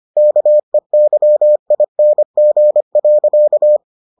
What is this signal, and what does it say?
Morse code, heard as a keyed tone, about 25 wpm, 600 Hz. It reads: KEYING.